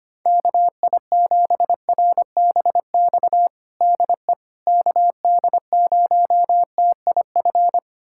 Morse code, heard as a keyed tone, about 25 wpm, 700 Hz. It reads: KI7RBX DE KD0TIF
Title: USA Callsign DE Callsign - 6 Characters - 25wpm